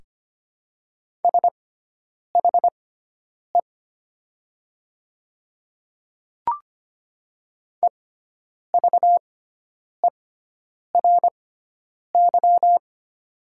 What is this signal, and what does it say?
Morse code, heard as a keyed tone, about 25 wpm, spaced as for 9 wpm, 690 Hz. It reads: SHE EVERY
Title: 6x Character Spacing - Top 200 Words - 25wpm